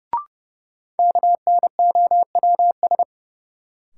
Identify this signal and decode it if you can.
Morse code, heard as a keyed tone, about 30 wpm, 700 Hz. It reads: KNOWS
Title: Sentences from Top 1000 Words - Review Part 1 - New Word Drill - 30wpm